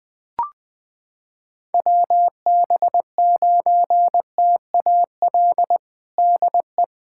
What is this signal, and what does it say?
Morse code, heard as a keyed tone, about 20 wpm, 700 Hz. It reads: WB9TAL DE
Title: USA Callsign DE Callsign - 6 Characters - 20wpm